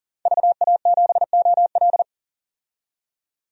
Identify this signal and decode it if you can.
Morse code, heard as a keyed tone, about 40 wpm, 700 Hz. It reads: VA7OL
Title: DXCC-100 Call Signs - 5 Characters - 40wpm